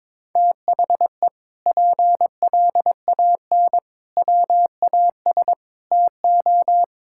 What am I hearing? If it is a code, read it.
Morse code, heard as a keyed tone, about 22 wpm, 700 Hz. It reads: THE PLAN WAS TO
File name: practice / 22wpm / Sentences with 8-Word Phrases - 22wpm